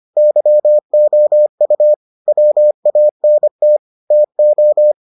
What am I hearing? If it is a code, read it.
Morse code, heard as a keyed tone, about 25 wpm, 600 Hz. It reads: YOU WANT TO